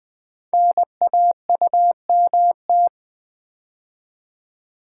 Morse code, heard as a keyed tone, about 20 wpm, 700 Hz. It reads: NAUMT
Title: Code Groups - 5 Letters - 20wpm